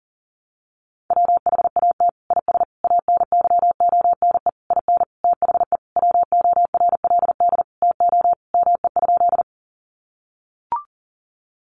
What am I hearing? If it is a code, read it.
Morse code, heard as a keyed tone, about 40 wpm, 700 Hz. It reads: WHAT IS ANYONE IN THE WORLD TO ME?
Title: Sentences from Top 900 Words 40wpm